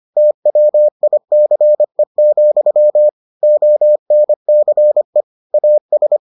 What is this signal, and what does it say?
Morse code, heard as a keyed tone, about 25 wpm, 600 Hz. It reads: TWICE, ONCE AS